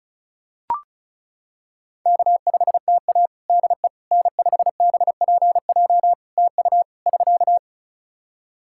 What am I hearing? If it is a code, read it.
Morse code, heard as a keyed tone, about 34 wpm, 700 Hz. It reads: K5TA DE N5BPJ TU <SK>
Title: K1USN SST QSO Practice 35wpm